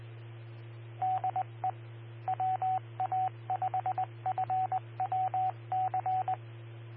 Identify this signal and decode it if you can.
Morse code, heard as a keyed tone, about 20 wpm, 750 Hz, with radio noise. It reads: DE WA5FWC
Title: Austin, TX Amateur Repeaters